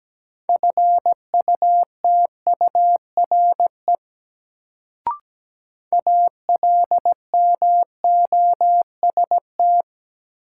Morse code, heard as a keyed tone, about 17 wpm, 700 Hz. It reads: FUTURE ALMOST
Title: Sentences from Top 700 Words - New Word Drill - 17wpm